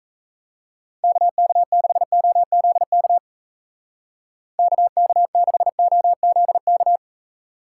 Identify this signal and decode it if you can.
Morse code, heard as a keyed tone, about 38 wpm, 700 Hz. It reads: KK6OZK KK6OZK